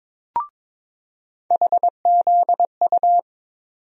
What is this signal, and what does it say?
Morse code, heard as a keyed tone, about 22 wpm, 700 Hz. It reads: HZU
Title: USA Call Sign Postfixes - 3 Letters 22wpm